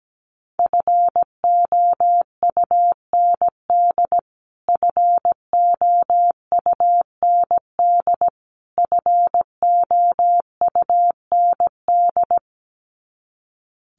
Morse code, heard as a keyed tone, about 17 wpm, 700 Hz. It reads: FOUND FOUND FOUND
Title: Top 500 Words Repeated 17wpm